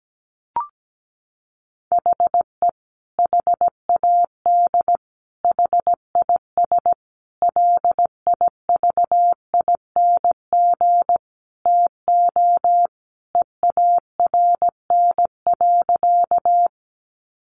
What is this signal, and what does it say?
Morse code, heard as a keyed tone, about 17 wpm, 700 Hz. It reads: HE HAD HIS LIVING TO EARN.
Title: Sentences from Top 1500 Words 17wpm